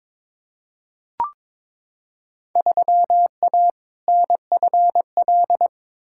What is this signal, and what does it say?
Morse code, heard as a keyed tone, about 22 wpm, 700 Hz. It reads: 3A NFL
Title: Field Day Exchanges - 22wpm